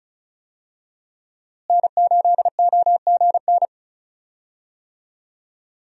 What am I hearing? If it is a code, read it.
Morse code, heard as a keyed tone, about 36 wpm, 700 Hz. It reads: N8OGN
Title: USA Call Signs - 5 Characters - Repeated 2x - 35wpm